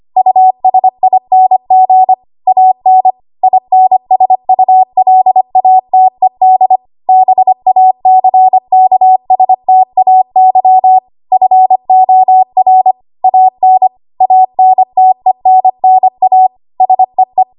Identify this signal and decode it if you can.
Morse code, heard as a keyed tone, about 25 wpm, 750 Hz. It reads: USING AN INSULATED BACKSTAY FOR AN ANTENNA SEE
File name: Archive / 25 WPM / 25WPM